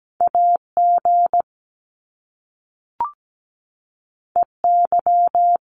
Morse code, heard as a keyed tone, about 17 wpm, 700 Hz. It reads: AG EY
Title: Top Two Letter Combinations in Words 17wpm